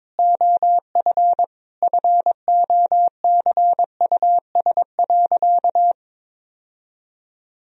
Morse code, heard as a keyed tone, about 22 wpm, 700 Hz. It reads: OF FOCUS.
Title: Sentences from Top 1000 Words - Review Part 4 - 22wpm